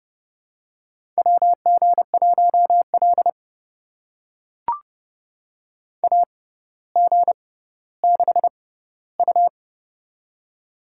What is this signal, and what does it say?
Morse code, heard as a keyed tone, about 30 wpm, spaced as for 15 wpm, 700 Hz. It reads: WG1L AG6U
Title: USA Call Signs - 4 Characters - 6x Character Spacing - 30wpm